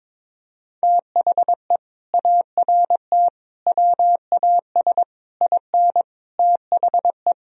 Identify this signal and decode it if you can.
Morse code, heard as a keyed tone, about 22 wpm, 700 Hz. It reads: THE ART WAS IN THE
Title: Sentences from Top 1000 Words - Review Part 3 - 22wpm